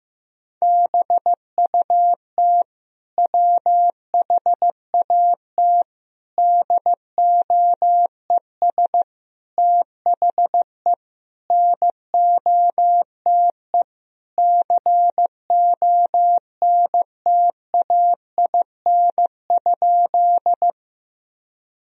Morse code, heard as a keyed tone, about 16 wpm, 700 Hz. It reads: BUT WHAT DOES THE NOTE CONTAIN?
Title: Sentences from Top 1000 Words - Review Part 2 - 15wpm